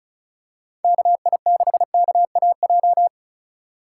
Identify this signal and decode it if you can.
Morse code, heard as a keyed tone, about 36 wpm, 700 Hz. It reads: KI6KAJ
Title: USA Call Signs - 6 Characters - 3x Character Spacing - 35wpm